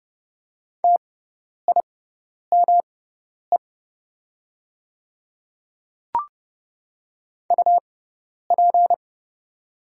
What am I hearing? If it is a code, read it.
Morse code, heard as a keyed tone, about 30 wpm, spaced as for 10 wpm, 700 Hz. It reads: TIME UP
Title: 6x Character Spacing - Top 100 Words - 30wpm